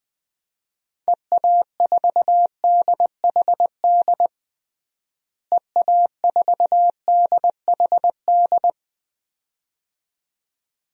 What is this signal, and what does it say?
Morse code, heard as a keyed tone, about 20 wpm, 700 Hz. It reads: EA4DHD EA4DHD
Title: DXCC-100 Call Signs - 6 Characters - Repeated 2x - 20wpm